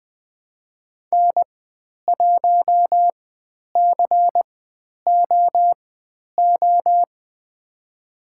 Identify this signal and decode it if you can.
Morse code, heard as a keyed tone, about 20 wpm, spaced as for 10 wpm, 700 Hz. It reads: N1COO
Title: 095 Lesson 16g - Letter c - Call Signs 20wpm